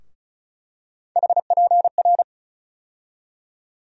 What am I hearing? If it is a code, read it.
Morse code, heard as a keyed tone, about 34 wpm, 700 Hz. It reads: HPR
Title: USA Call Sign Postfixes - 3 Letters 35wpm